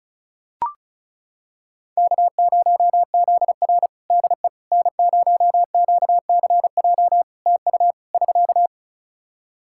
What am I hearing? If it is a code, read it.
Morse code, heard as a keyed tone, about 34 wpm, 700 Hz. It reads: K0ZR DE N0QCJ TU <SK>
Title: Practice QSO - Exchange 4 - Ending - 35wpm